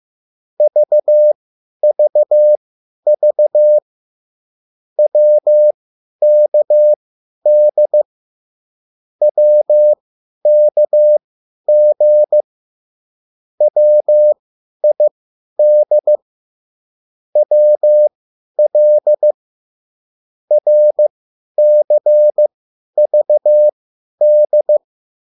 Morse code, heard as a keyed tone, about 15 wpm, spaced as for 10 wpm, 600 Hz. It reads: VVV WKD WKG WID WL RCVD
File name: QSL / CW lesson